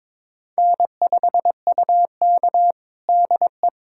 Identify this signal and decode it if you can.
Morse code, heard as a keyed tone, about 23 wpm, 700 Hz. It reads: N5UK DE
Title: USA Callsign DE Callsign - 4 Characters - 22wpm